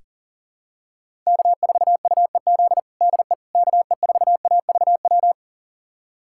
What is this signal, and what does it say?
Morse code, heard as a keyed tone, about 37 wpm, 700 Hz. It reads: K4UEZ DE KE4AVW
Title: USA Callsign DE Callsign 40wpm